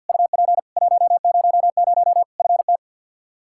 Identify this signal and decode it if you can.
Morse code, heard as a keyed tone, about 47 wpm, 700 Hz. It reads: UP 100 FT
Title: Practice QSO - Exchange 2 - Info - 50wpm